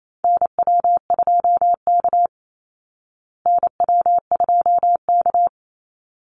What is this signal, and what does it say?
Morse code, heard as a keyed tone, about 28 wpm, 690 Hz. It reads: NW2K NW2K
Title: Call Signs Repeated 3x - 28wpm